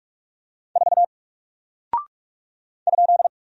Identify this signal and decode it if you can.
Morse code, heard as a keyed tone, about 48 wpm, 700 Hz. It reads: 4 ?
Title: Single Character (Letter:Number:Punctuation:Prosign) Alt 45wpm